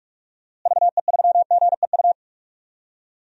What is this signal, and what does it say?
Morse code, heard as a keyed tone, about 46 wpm, 700 Hz. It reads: VE3GEU
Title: DXCC-100 Call Signs - 6 Characters - Repeated 3x - 45wpm